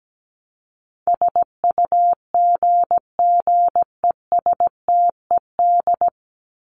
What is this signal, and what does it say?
Morse code, heard as a keyed tone, about 17 wpm, 700 Hz. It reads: SUGGESTED
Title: ED Suffix - Top 100 Words with Base Words 17wpm